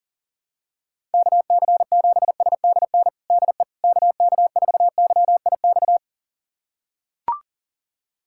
Morse code, heard as a keyed tone, about 37 wpm, 700 Hz. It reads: KC7SDN DE KK4YIX
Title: USA Callsign DE Callsign - 6 Characters - 40wpm